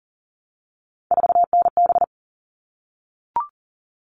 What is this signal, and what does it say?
Morse code, heard as a keyed tone, about 40 wpm, 700 Hz. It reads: 4NB
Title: All Cut Numbers - 3-Digit 40wpm